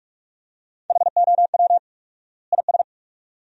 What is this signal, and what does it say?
Morse code, heard as a keyed tone, about 47 wpm, 700 Hz. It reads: HOW IS
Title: 4x Word-Spacing - Sentences from Top 100 Words - 45wpm